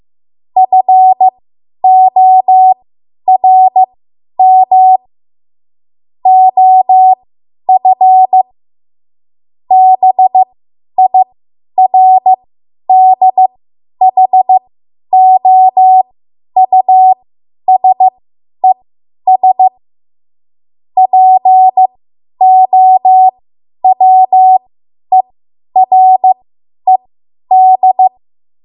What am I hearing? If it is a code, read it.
Morse code, heard as a keyed tone, about 15 wpm, spaced as for 10 wpm, 750 Hz. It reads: FORM OF BIRDHOUSES POWERED
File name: Morse / 10WPM